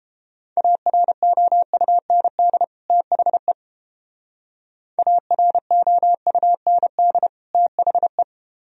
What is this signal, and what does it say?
Morse code, heard as a keyed tone, about 29 wpm, 700 Hz. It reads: AROUND THE AROUND THE